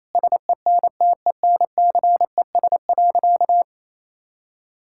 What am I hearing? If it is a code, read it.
Morse code, heard as a keyed tone, about 28 wpm, 700 Hz. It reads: SENTENCES.
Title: Pangram Challenge v2 - 28wpm